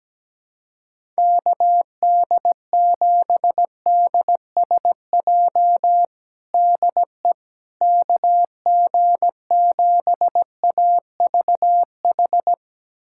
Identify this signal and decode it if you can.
Morse code, heard as a keyed tone, about 17 wpm, 700 Hz. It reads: KD7DSJ DE KG7AVH
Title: USA Callsign DE Callsign 17wpm